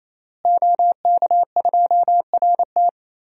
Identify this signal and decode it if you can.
Morse code, heard as a keyed tone, about 27 wpm, 700 Hz. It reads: OK2RT
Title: DXCC-100 Call Signs - 5 Characters - Repeated 3x - 28wpm